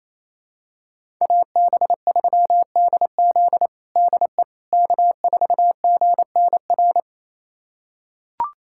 Morse code, heard as a keyed tone, about 28 wpm, 700 Hz. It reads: AB3DZ DE K4GNR
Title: USA Callsign DE Callsign - 5 Characters - 28wpm